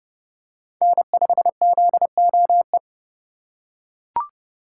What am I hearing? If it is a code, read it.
Morse code, heard as a keyed tone, about 29 wpm, 700 Hz. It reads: N5ZOE